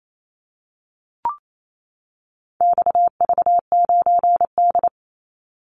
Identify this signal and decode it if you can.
Morse code, heard as a keyed tone, about 28 wpm, 700 Hz. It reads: XV9D
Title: Call Signs - 28wpm